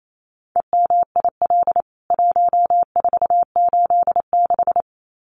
Morse code, heard as a keyed tone, about 28 wpm, 700 Hz. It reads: EMIL 1486
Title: CWT Exchange 28wpm